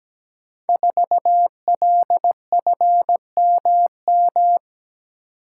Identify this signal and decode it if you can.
Morse code, heard as a keyed tone, about 17 wpm, 700 Hz. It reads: 4LFMM
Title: Code Groups - 5 Letter:Numbers - 17wpm